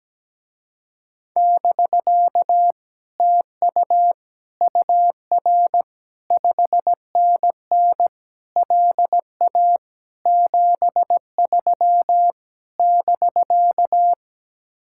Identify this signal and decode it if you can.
Morse code, heard as a keyed tone, about 17 wpm, 700 Hz. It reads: <BK> TU UR 5NN LA 73 <BK>
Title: Speed-Racing Common QSO Phrases - 17wpm